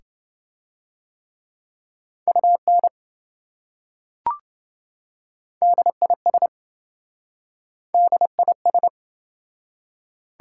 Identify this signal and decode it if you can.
Morse code, heard as a keyed tone, about 26 wpm, 700 Hz. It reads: UN DIS DIS